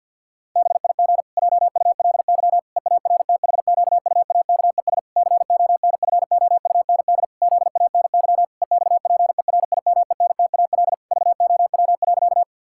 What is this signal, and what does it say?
Morse code, heard as a keyed tone, about 52 wpm, 700 Hz. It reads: BIG JULY EARTHQUAKES CONFOUND ZANY EXPERIMENTAL VOW.
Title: Pangram Challenge v2 - 50wpm